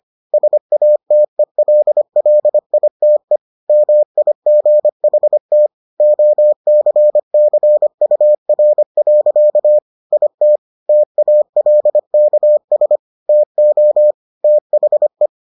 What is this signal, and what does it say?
Morse code, heard as a keyed tone, about 25 wpm, 600 Hz. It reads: SATELLITE MIGHT OCCUR. IT TALKS TO THE